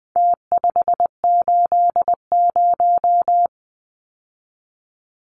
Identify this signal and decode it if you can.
Morse code, heard as a keyed tone, about 20 wpm, 700 Hz. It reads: T580
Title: Numbers - Contest Style - 4-Digit 20wpm